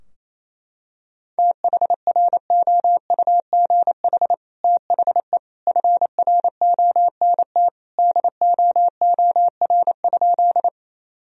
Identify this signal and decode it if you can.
Morse code, heard as a keyed tone, about 28 wpm, 700 Hz. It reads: THROUGH THE FRONT DOOR?
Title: Sentences from Top 1000 Words - Review Part 2 - 28wpm